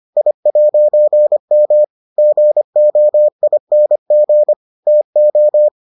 Morse code, heard as a keyed tone, about 25 wpm, 600 Hz. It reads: I'M GOING TO